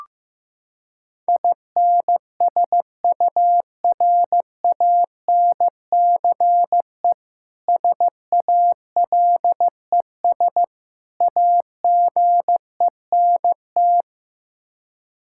Morse code, heard as a keyed tone, about 15 wpm, 700 Hz. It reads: INSURANCE SALES AGENT
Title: Practice QSO - Exchange 3 - More Info - 15wpm